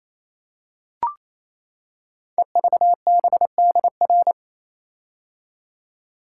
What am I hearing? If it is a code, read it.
Morse code, heard as a keyed tone, about 27 wpm, 700 Hz. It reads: EVBDR
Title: Code Groups - 5 Letter:Numbers - 28wpm